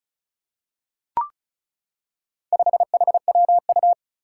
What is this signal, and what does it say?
Morse code, heard as a keyed tone, about 36 wpm, 700 Hz. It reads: 5HWU
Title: Four Challenging Letter-Number Combinations 35wpm